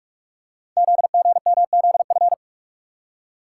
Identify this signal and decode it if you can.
Morse code, heard as a keyed tone, about 44 wpm, 700 Hz. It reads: ZGMZF
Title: Code Groups - 5 Letters - 45wpm